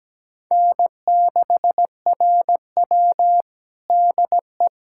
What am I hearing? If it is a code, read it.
Morse code, heard as a keyed tone, about 17 wpm, 700 Hz. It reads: N6RW DE
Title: USA Callsign DE Callsign - 4 Characters - 17wpm